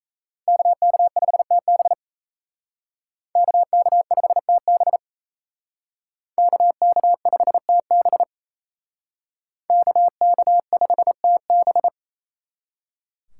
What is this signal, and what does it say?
Morse code, heard as a keyed tone, about 35 wpm, 700 Hz. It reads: KK5TB KK5TB KK5TB KK5TB
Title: Speed Racing - USA Call Signs 28wpm